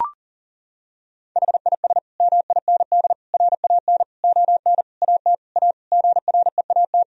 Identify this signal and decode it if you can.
Morse code, heard as a keyed tone, about 37 wpm, 700 Hz. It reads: HIS MIND RAN ON AT A GREAT
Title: Sentences from Top 1000 Words - Review Part 4 - 40wpm